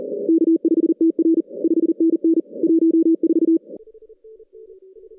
Morse code, heard as a keyed tone, about 40 wpm, 330 Hz. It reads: K5TR 5NN 14
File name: 2015 cq ww cw / 2015 cqwww k5tr - 0-41